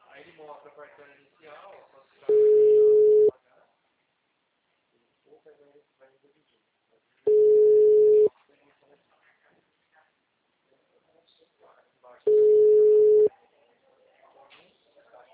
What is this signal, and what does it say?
Ringback tone (Europe, 425 Hz), with speech.